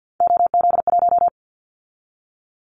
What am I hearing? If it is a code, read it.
Morse code, heard as a keyed tone, about 50 wpm, 700 Hz. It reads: OZ1
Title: Callsign Prefixes for Top 50 DXCC - 50wpm